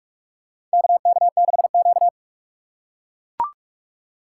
Morse code, heard as a keyed tone, about 43 wpm, 700 Hz. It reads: KK6Q
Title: USA Call Signs - 4 Characters - 8x Character Spacing - 45wpm